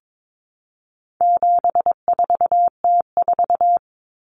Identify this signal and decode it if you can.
Morse code, heard as a keyed tone, about 22 wpm, 700 Hz. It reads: 74T4
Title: Common Cut Numbers - 4-Digit 22wpm